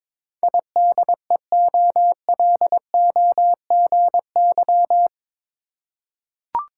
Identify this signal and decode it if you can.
Morse code, heard as a keyed tone, about 22 wpm, 700 Hz. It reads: IDEOLOGY
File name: practice / 22wpm / Eight Letter Words - 22wpm